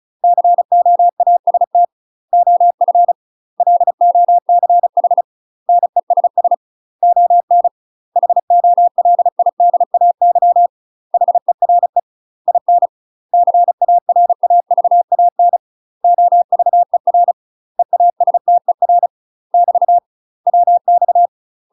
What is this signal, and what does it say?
Morse code, heard as a keyed tone, about 34 wpm, 700 Hz. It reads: COAST OF LOCH NESS ON HOLIDAY HERE IN CARAVAN OVER EASTER <BT> WX
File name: UK MASTER QSOs TRAINING / 35WPM / UK MASTER QSOs 0015